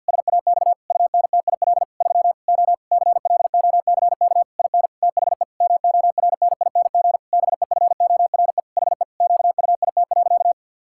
Morse code, heard as a keyed tone, about 50 wpm, 700 Hz. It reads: SAX UNTIL 3 O CLOCK IN THE MORNING BEFORE HE QUIT.
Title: Pangram Challenge v2 - 50wpm